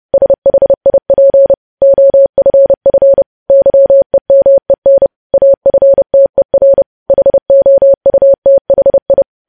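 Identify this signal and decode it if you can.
Morse code, heard as a keyed tone, about 30 wpm, 550 Hz. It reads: SHIP OFF YEMEN AFTER HOUTHI
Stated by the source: Morse code news headlines practice at 30 WPM.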